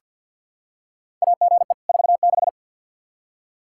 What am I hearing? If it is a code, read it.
Morse code, heard as a keyed tone, about 49 wpm, 700 Hz. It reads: AGE 46
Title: Practice QSO - Exchange 3 - More Info - 50wpm